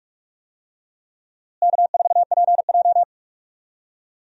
Morse code, heard as a keyed tone, about 44 wpm, 700 Hz. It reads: K4PJ
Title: USA Call Signs - 4 Characters - Repeated 3x - 45wpm